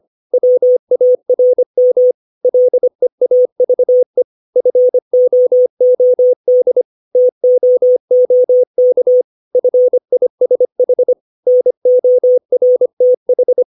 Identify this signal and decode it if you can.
Morse code, heard as a keyed tone, about 25 wpm, 500 Hz. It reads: WARM LEAVE FOOD TOOK FISH NORTH
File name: 632 words / 632 Words 25WPM